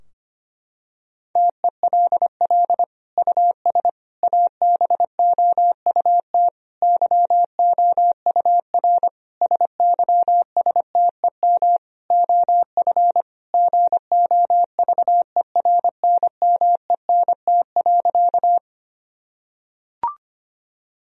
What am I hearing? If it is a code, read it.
Morse code, heard as a keyed tone, about 25 wpm, 700 Hz. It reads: TELL US ABOUT YOUR SYSTEM OF GOVERNMENT.
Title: Sentences from Top 1000 Words - Review Part 3 - 25wpm